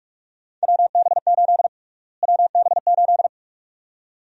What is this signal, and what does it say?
Morse code, heard as a keyed tone, about 42 wpm, 700 Hz. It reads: WB8 WB8
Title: USA Call Sign Prefixes - Repeated 2x - 45wpm